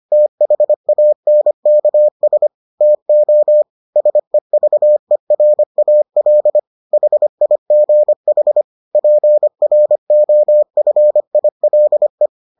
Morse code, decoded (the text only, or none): THANKS TO SEVERAL HIGH PROFILE